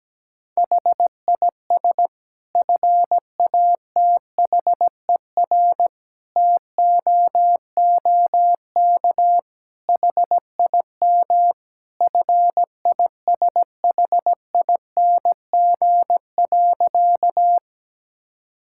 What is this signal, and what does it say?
Morse code, heard as a keyed tone, about 17 wpm, 700 Hz. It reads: HIS FATHER TOOK HIM FISHING.
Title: Sentences from Top 1000 Words - Review Part 5 - 17wpm